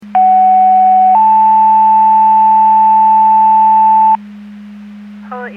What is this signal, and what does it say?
Two-tone page, heard as a one-second tone, then a three-second tone, about 750 then 880 Hz, with speech. It is heard from 0.2 to 4.2 s.